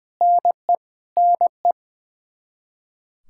Morse code, heard as a keyed tone, about 20 wpm, 700 Hz. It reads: NE NE
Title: Practice QSO - Exchange 1 - Beginning - 20wpm